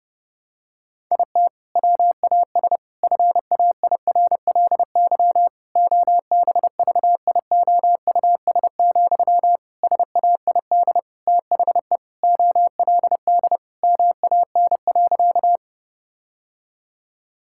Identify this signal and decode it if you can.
Morse code, heard as a keyed tone, about 30 wpm, 700 Hz. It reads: IT WAS FAIRLY OBVIOUS, SAID THE OLD MAN.
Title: Sentences from Top 1600 Words 30wpm